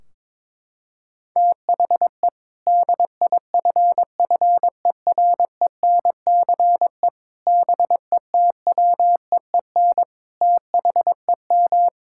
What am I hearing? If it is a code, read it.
Morse code, heard as a keyed tone, about 22 wpm, 700 Hz. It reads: THE DIFFERENCE BETWEEN THEM
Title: Sentences from Top 1600 Words 22wpm